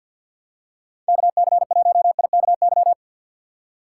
Morse code, heard as a keyed tone, about 46 wpm, 700 Hz. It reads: KC1IKY
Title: USA Call Signs - 6 Characters - 4x Character Spacing - 50wpm